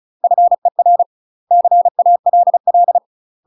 Morse code, heard as a keyed tone, about 34 wpm, 700 Hz. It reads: FER CALL
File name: UK MASTER QSOs TRAINING / 35WPM / UK MASTER QSOs 0041